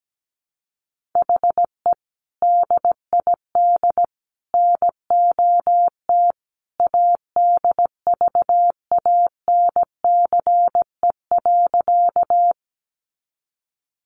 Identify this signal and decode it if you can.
Morse code, heard as a keyed tone, about 17 wpm, 700 Hz. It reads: HE DID NOT ADVANCE.
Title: Sentences from Top 1400 Words 17wpm